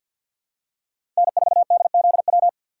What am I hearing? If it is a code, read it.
Morse code, heard as a keyed tone, about 51 wpm, 700 Hz. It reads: N4DZW
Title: USA Call Signs - 5 Characters - 3x Character Spacing - 50wpm